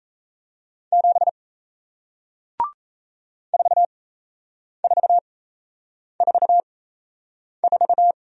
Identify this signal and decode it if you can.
Morse code, heard as a keyed tone, about 38 wpm, spaced as for 8 wpm, 700 Hz. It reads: 7 4444